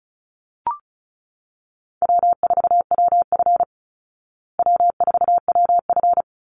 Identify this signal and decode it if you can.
Morse code, heard as a keyed tone, about 34 wpm, 700 Hz. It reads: W4WF W4WF
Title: Easy Call Signs Repeated 3x - 35wpm